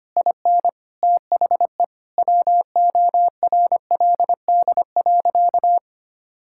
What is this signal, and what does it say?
Morse code, heard as a keyed tone, about 25 wpm, 700 Hz. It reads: IN THE WORLD.
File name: practice / 25wpm / Sentences with 9-Word Phrases - 25wpm